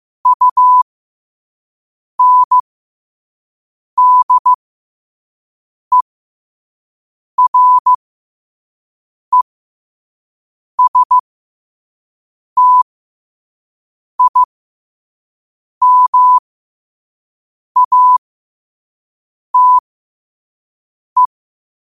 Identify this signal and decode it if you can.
Morse code, heard as a keyed tone, about 15 wpm, spaced as for 5 wpm, 1000 Hz. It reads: UNDERESTIMATE
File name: qotdfiles / 5wpm